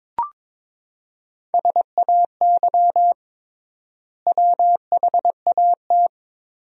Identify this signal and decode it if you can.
Morse code, heard as a keyed tone, about 22 wpm, 700 Hz. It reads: SAY WHAT
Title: 3x Word-Spacing - Sentences from Top 200 Words - 22wpm